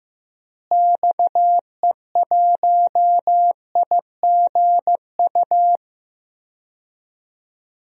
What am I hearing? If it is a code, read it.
Morse code, heard as a keyed tone, about 15 wpm, 700 Hz. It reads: XE1IGU
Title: DXCC-100 Call Signs - 6 Characters - Repeated 3x - 15wpm